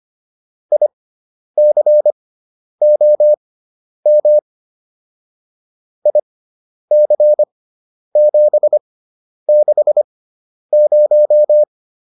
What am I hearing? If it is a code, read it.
Morse code, heard as a keyed tone, about 25 wpm, spaced as for 10 wpm, 600 Hz. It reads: ICOM IC760